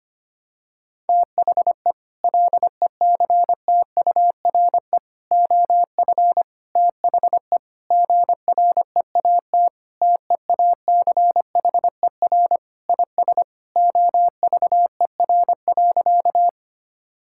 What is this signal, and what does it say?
Morse code, heard as a keyed tone, about 25 wpm, 700 Hz. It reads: THE LECTURE OF THE GREAT TEACHER IS OVER.
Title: Sentences from Top 1600 Words 25wpm